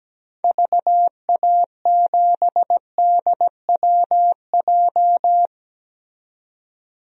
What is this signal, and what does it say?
Morse code, heard as a keyed tone, about 17 wpm, 700 Hz. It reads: VA7DWJ
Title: DXCC-100 Call Signs - 6 Characters - 17wpm